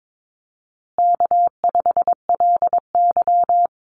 Morse code, heard as a keyed tone, about 22 wpm, 700 Hz. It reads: K5LY
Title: Easy Call Signs Repeated 3x - 22wpm